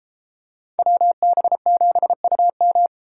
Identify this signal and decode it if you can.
Morse code, heard as a keyed tone, about 32 wpm, 700 Hz. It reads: WB7UM